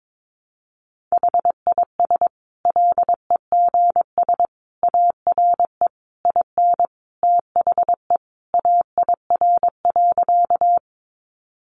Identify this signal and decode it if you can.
Morse code, heard as a keyed tone, about 22 wpm, 700 Hz. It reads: HIS LEGS ARE IN THE AIR.
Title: Additional Easy Phrases 22wpm